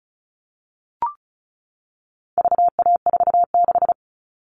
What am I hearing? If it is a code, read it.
Morse code, heard as a keyed tone, about 34 wpm, 700 Hz. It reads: VA46
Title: All Cut Numbers - 4-Digit 35wpm